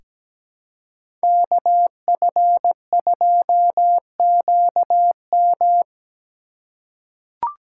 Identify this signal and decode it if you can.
Morse code, heard as a keyed tone, about 17 wpm, 700 Hz. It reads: KF2QM
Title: USA Call Signs - 5 Characters - 4x Character Spacing - 17wpm